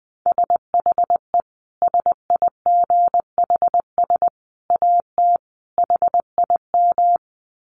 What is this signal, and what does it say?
Morse code, heard as a keyed tone, about 20 wpm, 700 Hz. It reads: SHE SIGHS AT HIM
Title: Dit Challenge 20wpm